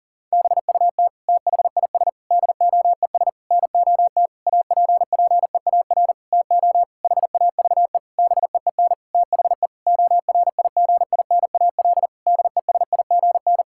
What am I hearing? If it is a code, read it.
Morse code, heard as a keyed tone, about 40 wpm, 700 Hz. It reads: BUT THIS DOES NOT APPEAR TO HAVE BEEN THE ORIGINAL DESIGN
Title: Sentences from Top 1000 Words - Review Part 6 - 40wpm